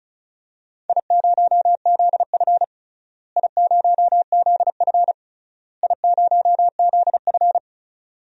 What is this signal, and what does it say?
Morse code, heard as a keyed tone, about 36 wpm, 700 Hz. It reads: I0ZF I0ZF I0ZF
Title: DXCC-100 Call Signs - 4 Characters - Repeated 3x - 35wpm